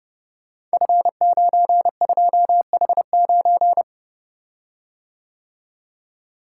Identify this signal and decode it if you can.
Morse code, heard as a keyed tone, about 30 wpm, 700 Hz. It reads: F92H9
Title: Code Groups - 5 Letter:Numbers - 30wpm